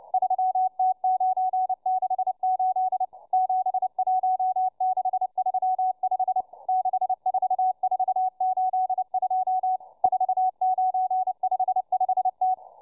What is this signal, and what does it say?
Morse code, heard as a keyed tone, about 29 wpm, 750 Hz, with radio noise. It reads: 3T968 71635 64482 4955T